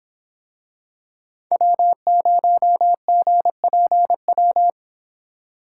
Morse code, heard as a keyed tone, about 26 wpm, 700 Hz. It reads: W0GPW